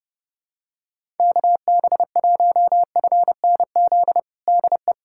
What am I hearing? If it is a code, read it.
Morse code, heard as a keyed tone, about 32 wpm, 690 Hz. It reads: KB1FNZ DE
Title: USA Callsign DE Callsign - 6 Characters - 30wpm